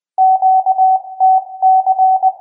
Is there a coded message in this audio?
Yes: Morse code from 0.2 to 2.3 s, heard as a keyed tone, about 20 wpm, 760 Hz. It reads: QTC